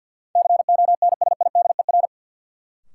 Morse code, heard as a keyed tone, about 53 wpm, 700 Hz. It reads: CONSIDER
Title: Sentences from Top 600 Words - New Word Drill - 50wpm